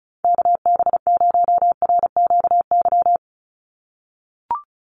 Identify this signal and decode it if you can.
Morse code, heard as a keyed tone, about 34 wpm, 700 Hz. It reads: KB0RQY
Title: USA Call Signs 35wpm